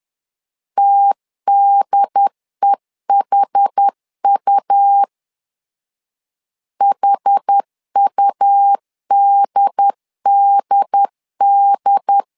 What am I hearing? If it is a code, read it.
Morse code, heard as a keyed tone, about 10 wpm, 800 Hz. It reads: TDEHU HUDDD
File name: LEZIONI / 2 UHDN / 2 UHD 10wpm 1826